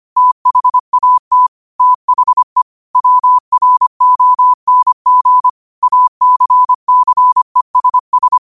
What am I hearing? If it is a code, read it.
Morse code, heard as a keyed tone, about 25 wpm, 1000 Hz. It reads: THAT THE WRONG ACCESS